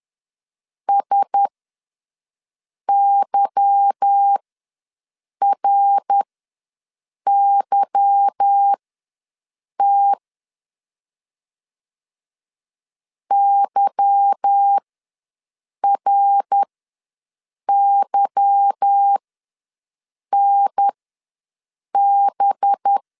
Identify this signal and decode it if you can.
Morse code, heard as a keyed tone, about 10 wpm, spaced as for 6 wpm, 800 Hz. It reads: SYRYT YRYNB
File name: LEZIONI / 8 YJ45 / 8 Y ws 10wpm 3952